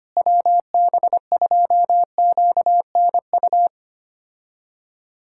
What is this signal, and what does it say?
Morse code, heard as a keyed tone, about 25 wpm, 700 Hz. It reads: WB2QNU
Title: K1USN SST QSO Practice - 25wpm